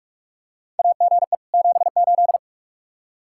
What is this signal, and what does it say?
Morse code, heard as a keyed tone, about 45 wpm, 700 Hz. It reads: AGE 78
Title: K1USN SST QSO Practice 45wpm